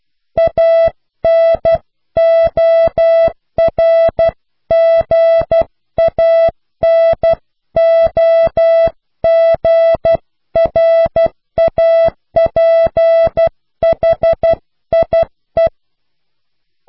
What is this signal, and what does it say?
Morse code, heard as a keyed tone, about 12 wpm, 660 Hz. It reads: ANORGANOGRAPHIE
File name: cw / Pseudo woorden (15)